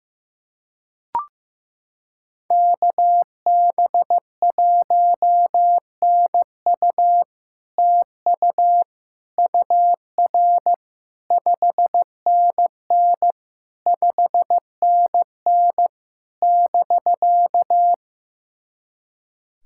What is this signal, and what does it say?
Morse code, heard as a keyed tone, about 15 wpm, 700 Hz. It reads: KB1NU TU UR 5NN 5NN <BK>
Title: SOTA QSO Practice 15wpm